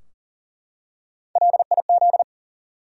Morse code, heard as a keyed tone, about 40 wpm, 700 Hz. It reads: LIZ
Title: USA Common Names - Max 3 Letters - 40wpm